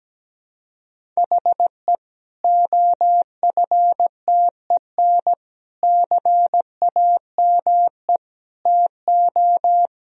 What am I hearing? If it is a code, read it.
Morse code, heard as a keyed tone, about 17 wpm, 700 Hz. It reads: HE OFTEN CAME TO
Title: Sentences from Top 1000 Words - Review Part 1 - 17wpm